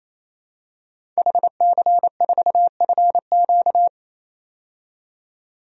Morse code, heard as a keyed tone, about 27 wpm, 700 Hz. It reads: HC4FQ
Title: DXCC-100 Call Signs - 5 Characters - 28wpm